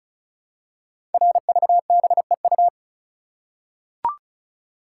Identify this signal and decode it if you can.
Morse code, heard as a keyed tone, about 36 wpm, 700 Hz. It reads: RVBEU
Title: Code Groups - 5 Letters - 35wpm